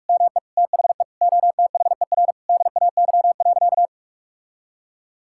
Morse code, heard as a keyed tone, about 41 wpm, 700 Hz. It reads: ME THE OTHER DAY.
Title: Sentences from Top 1000 Words - Review Part 5 - 45wpm